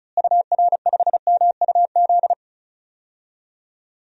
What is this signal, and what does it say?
Morse code, heard as a keyed tone, about 34 wpm, 700 Hz. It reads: UR5MUZ
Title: DXCC-100 Call Signs - 6 Characters - Repeated 2x - 35wpm